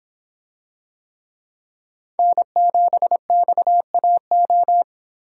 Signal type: Morse code

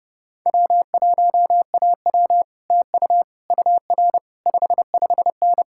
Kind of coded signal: Morse code

